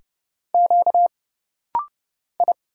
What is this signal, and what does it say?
Morse code, heard as a keyed tone, about 30 wpm, 700 Hz. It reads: Q I
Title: Single Letter-Number - Rapid-Fire 30wpm